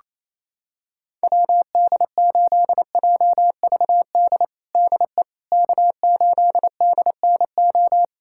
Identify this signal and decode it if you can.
Morse code, heard as a keyed tone, about 28 wpm, 700 Hz. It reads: WD8JVD DE K8DNO
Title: USA Callsign DE Callsign 28wpm